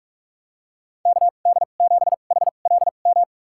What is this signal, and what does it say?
Morse code, heard as a keyed tone, about 46 wpm, spaced as for 33 wpm, 700 Hz. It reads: KD7HLM
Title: USA Call Signs - 6 Characters - 2x Character Spacing - 45wpm